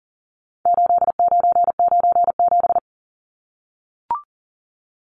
Morse code, heard as a keyed tone, about 40 wpm, 700 Hz. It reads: 8997
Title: Numbers - 4-Digit 40wpm